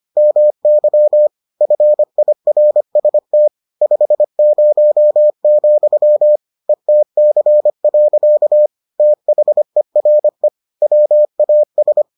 Morse code, heard as a keyed tone, about 25 wpm, 600 Hz. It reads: MY FIRST 50, ETC. THERE WAS